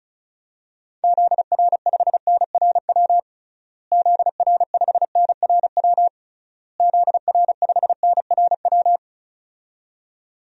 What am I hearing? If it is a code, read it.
Morse code, heard as a keyed tone, about 34 wpm, 700 Hz. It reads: ZR5NRW ZR5NRW ZR5NRW
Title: DXCC-100 Call Signs - 6 Characters - Repeated 3x - 35wpm